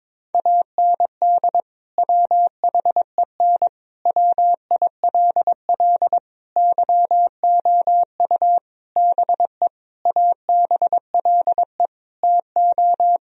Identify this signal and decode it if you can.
Morse code, heard as a keyed tone, about 22 wpm, 700 Hz. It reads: AND WHEN WILL YOU BE ABLE TO
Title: Sentences from Top 1000 Words - Review Part 2 - 22wpm